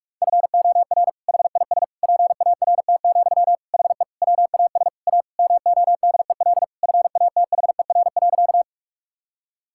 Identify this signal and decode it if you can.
Morse code, heard as a keyed tone, about 46 wpm, 700 Hz. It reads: FOR HIS PART, HE WAS A MODEL FATHER.
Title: Sentences from Top 1000 Words - Review Part 5 - 45wpm